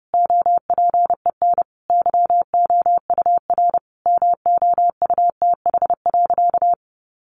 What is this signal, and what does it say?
Morse code, heard as a keyed tone, about 30 wpm, 700 Hz. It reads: OPEN YOUR MOUTH.
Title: Sentences from Top 1500 Words 30wpm